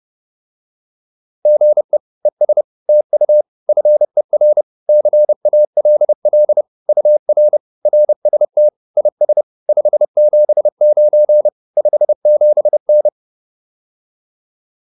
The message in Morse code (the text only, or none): GE ES TU FER CALL UR RST IS 579 57N